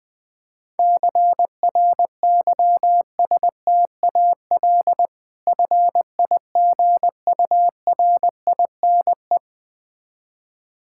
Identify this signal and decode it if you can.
Morse code, heard as a keyed tone, about 20 wpm, 690 Hz. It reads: CRYSTAL FIGURINE
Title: Difficult Words with Hints - 20wpm